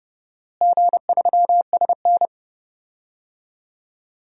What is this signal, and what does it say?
Morse code, heard as a keyed tone, about 32 wpm, 700 Hz. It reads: G3SN
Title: DXCC-100 Call Signs - 4 Characters - Repeated 2x - 30wpm